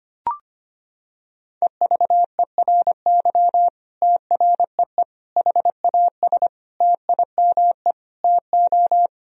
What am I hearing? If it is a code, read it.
Morse code, heard as a keyed tone, about 25 wpm, 700 Hz. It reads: EVERY TREE HAS TIME TO
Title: Sentences from Top 1000 Words - Review Part 1 - 25wpm